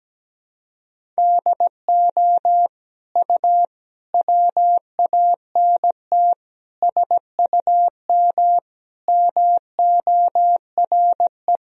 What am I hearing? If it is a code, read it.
Morse code, heard as a keyed tone, about 17 wpm, 700 Hz. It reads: DO U WANT SUM MORE
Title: Sentences from Top 300 Words and Abbreviations 17wpm